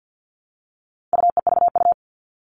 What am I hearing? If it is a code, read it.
Morse code, heard as a keyed tone, about 50 wpm, 700 Hz. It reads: UEVU
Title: All Cut Numbers - 4-Digit 50wpm